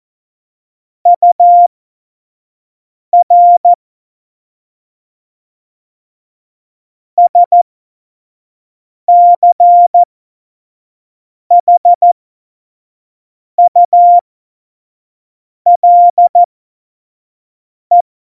Morse code, heard as a keyed tone, about 14 wpm, spaced as for 5 wpm, 700 Hz. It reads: UR SCHULE
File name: Morsekurs / Witze / Witz 20 March 2018, 25 F70